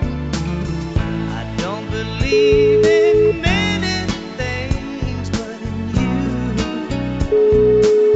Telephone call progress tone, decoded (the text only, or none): ringback tone